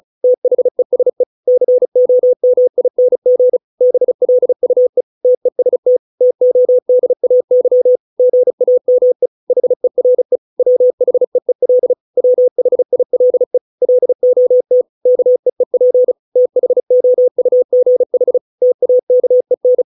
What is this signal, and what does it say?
Morse code, heard as a keyed tone, about 34 wpm, 500 Hz. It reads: THESE COMING BLUE TEST TODAY GAME HERE WHEEL WHILE LOT KEEP THOUGH TAKEN